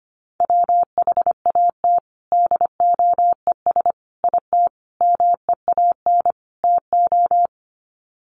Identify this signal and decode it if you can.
Morse code, heard as a keyed tone, about 25 wpm, 700 Hz. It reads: WHAT DOES IT MEAN TO
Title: Sets of 5 Words Encore2 25wpm